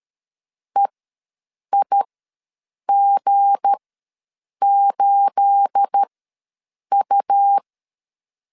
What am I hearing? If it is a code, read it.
Morse code, heard as a keyed tone, about 13 wpm, spaced as for 7 wpm, 800 Hz. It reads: EIG8U
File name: LEZIONI / 9 28 ws 12wpm 3354